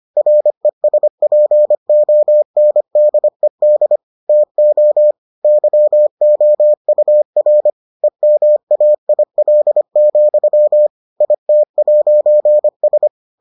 Morse code, heard as a keyed tone, about 25 wpm, 600 Hz. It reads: RESPONDED TO YOUR EMAIL, IT'S